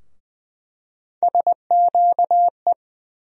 Morse code, heard as a keyed tone, about 20 wpm, 700 Hz. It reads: SQE
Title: USA Call Sign Postfixes - 3 Letters 20wpm